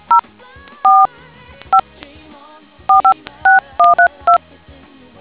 DTMF, heard as telephone touch tones, tones of about 110 ms. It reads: 0 4 5 456132